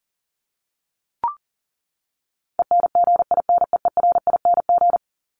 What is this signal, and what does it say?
Morse code, heard as a keyed tone, about 40 wpm, 700 Hz. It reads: ENGINEERING
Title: Top 1251-1500 Words 40wpm